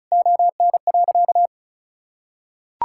Morse code, heard as a keyed tone, about 34 wpm, 700 Hz. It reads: ON.
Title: Sentences from Top 1000 Words - Review Part 5 - 35wpm